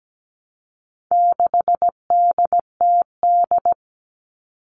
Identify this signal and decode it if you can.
Morse code, heard as a keyed tone, about 17 wpm, 700 Hz. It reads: 6DTD
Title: All Cut Numbers - 4-Digit 17wpm